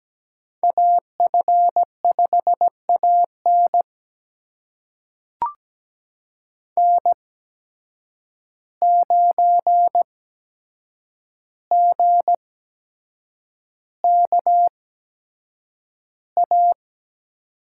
Morse code, heard as a keyed tone, about 17 wpm, spaced as for 7 wpm, 700 Hz. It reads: AF5AN N9GKA
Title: USA Call Signs - 5 Characters - 8x Character Spacing - 17wpm